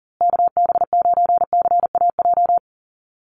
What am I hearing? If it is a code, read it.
Morse code, heard as a keyed tone, about 39 wpm, 700 Hz. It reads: KB9CAJ
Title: USA Call Signs Repeated 2x 40wpm